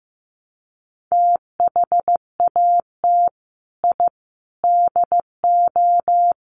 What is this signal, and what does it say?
Morse code, heard as a keyed tone, about 15 wpm, 700 Hz. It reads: THAT I DO
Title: Sets of 5 Words Encore 15wpm